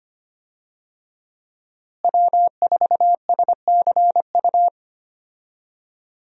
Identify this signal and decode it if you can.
Morse code, heard as a keyed tone, about 25 wpm, 700 Hz. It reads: W4SCU
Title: USA Call Signs - 5 Characters - Repeated 2x - 25wpm